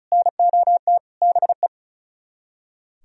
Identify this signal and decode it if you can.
Morse code, heard as a keyed tone, about 46 wpm, 700 Hz. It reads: NOT BE